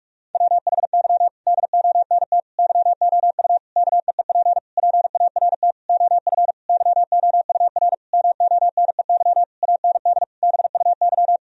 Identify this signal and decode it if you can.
Morse code, heard as a keyed tone, about 46 wpm, 700 Hz. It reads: WHY DONT YOU KEEP PART OF YOUR MONEY AND BUY
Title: Sentences from Top 1000 Words - Review Part 4 - 45wpm